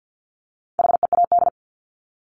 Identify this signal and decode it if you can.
Morse code, heard as a keyed tone, about 50 wpm, 700 Hz. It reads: HEAD
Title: Top 500 Words 50wpm